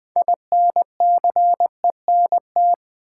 Morse code, heard as a keyed tone, about 20 wpm, 700 Hz. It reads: INCENT
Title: Build A Word - 3 Letters At A Time - 9-Letter Words - 20wpm